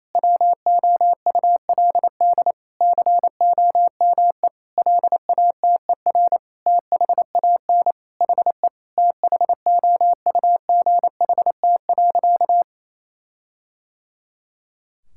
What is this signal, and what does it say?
Morse code, heard as a keyed tone, about 27 wpm, 700 Hz. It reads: WOULD COME LATER THAN HE THOUGHT.
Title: Sentences from Top 1600 Words 28wpm